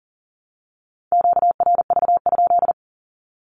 Morse code, heard as a keyed tone, about 40 wpm, 700 Hz. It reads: QRV?
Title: Q-Codes 40wpm